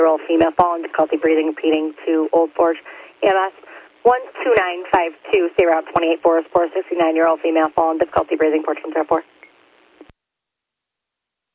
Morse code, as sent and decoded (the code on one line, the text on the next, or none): none